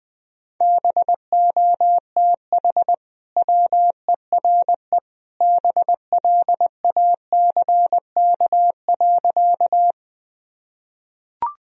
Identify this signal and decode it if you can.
Morse code, heard as a keyed tone, about 20 wpm, 700 Hz. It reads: BOTH WERE BLACK.
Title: Sentences from Top 1000 Words - Review Part 2 - 20wpm